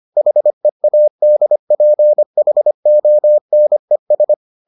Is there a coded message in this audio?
Yes: Morse code from 0.2 to 4.3 s, heard as a keyed tone, about 25 wpm, 600 Hz. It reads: HEADPHONES